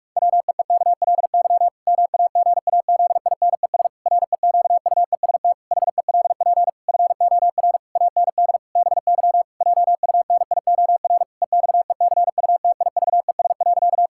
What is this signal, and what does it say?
Morse code, heard as a keyed tone, about 44 wpm, 700 Hz. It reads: WEEKLY MAGAZINES REQUEST HELP FOR AND BY JUNIOR EXECUTIVES.
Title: Pangram Challenge v2 - 45wpm